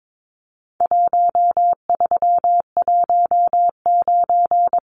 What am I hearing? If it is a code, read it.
Morse code, heard as a keyed tone, about 22 wpm, 700 Hz. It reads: 1319
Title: Numbers - Contest Style - 4-Digit 22wpm